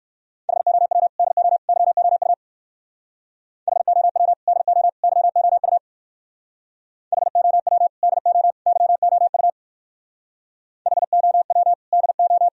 Morse code, heard as a keyed tone, about 58 wpm, 700 Hz. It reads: HOW DO YOU HOW DO YOU HOW DO YOU HOW DO